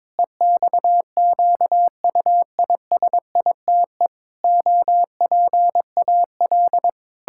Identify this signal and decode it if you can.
Morse code, heard as a keyed tone, about 22 wpm, 700 Hz. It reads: EXQUISITE OPAL
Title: Pangram Challenge v2 - 22wpm